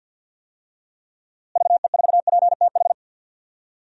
Morse code, heard as a keyed tone, about 46 wpm, 700 Hz. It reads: VE4PTH
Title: DXCC-100 Call Signs - 6 Characters - Repeated 2x - 50wpm